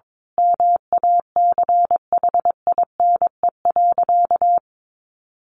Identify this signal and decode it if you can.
Morse code, heard as a keyed tone, about 22 wpm, 700 Hz. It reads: MACHINE.
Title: A Sound Of Thunder 22wpm